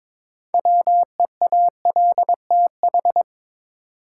Morse code, heard as a keyed tone, about 22 wpm, 700 Hz. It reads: WEALTH
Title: Sentences from Top 1200 Words - New Word Drill Alt - 22wpm